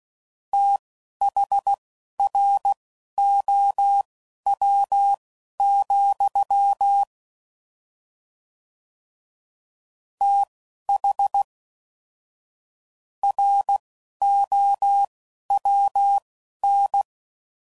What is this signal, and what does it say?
Morse code, heard as a keyed tone, about 16 wpm, spaced as for 11 wpm, 800 Hz. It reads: THROW, TH ROWN
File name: Prefixes / FLs & THs / Fast TH Prefixes